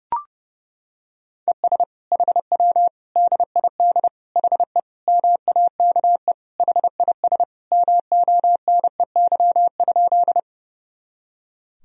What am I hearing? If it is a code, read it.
Morse code, heard as a keyed tone, about 30 wpm, 700 Hz. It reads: ES HW DID HE MAKE HIS MONEY?
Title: Sentences from Top 400 Words and Abbreviations 30wpm